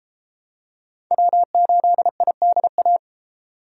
Morse code, heard as a keyed tone, about 32 wpm, 700 Hz. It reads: W8IDA